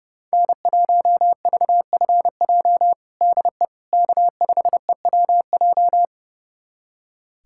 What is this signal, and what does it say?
Morse code, heard as a keyed tone, about 32 wpm, 700 Hz. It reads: N1VFJ DE K5EWJ
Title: USA Callsign DE Callsign - 5 Characters - 30wpm